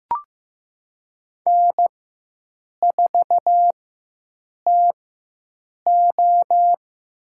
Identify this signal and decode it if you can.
Morse code, heard as a keyed tone, about 15 wpm, spaced as for 7 wpm, 700 Hz. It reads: N4TO
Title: USA Call Signs - 4 Characters - 4x Character Spacing - 15wpm